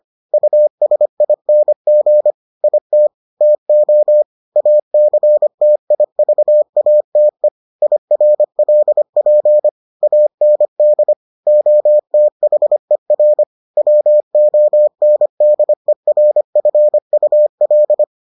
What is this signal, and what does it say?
Morse code, heard as a keyed tone, about 25 wpm, 600 Hz. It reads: USING IT TO ACTIVATE IRLP AND OTHER WONDERFUL